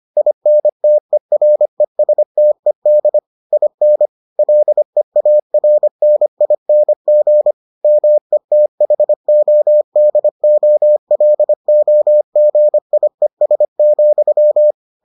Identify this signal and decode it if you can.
Morse code, heard as a keyed tone, about 25 wpm, 600 Hz. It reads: INTERESTED IN LEARNING METHODOLOGIES,